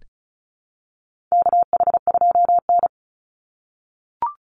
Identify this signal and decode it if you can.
Morse code, heard as a keyed tone, about 34 wpm, 700 Hz. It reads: KH2N
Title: Call Signs - 35wpm